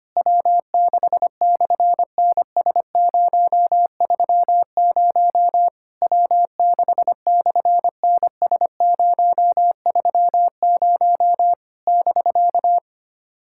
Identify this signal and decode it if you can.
Morse code, heard as a keyed tone, about 24 wpm, 700 Hz. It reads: W6/NS030 W6/NS030 <BK>
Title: SOTA QSO Practice 25wpm